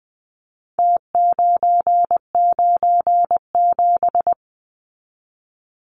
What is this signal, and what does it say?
Morse code, heard as a keyed tone, about 20 wpm, 700 Hz. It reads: T997
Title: Numbers - Contest Style - 4-Digit 20wpm